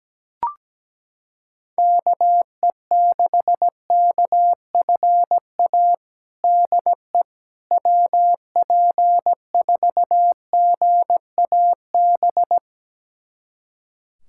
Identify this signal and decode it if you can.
Morse code, heard as a keyed tone, about 17 wpm, 700 Hz. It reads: KE6KFA DE WP4GAB
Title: USA Callsign DE Callsign - 6 Characters - 17wpm